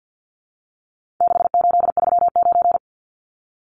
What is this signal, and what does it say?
Morse code, heard as a keyed tone, about 50 wpm, 700 Hz. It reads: 6839
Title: Numbers - 4-Digit 50wpm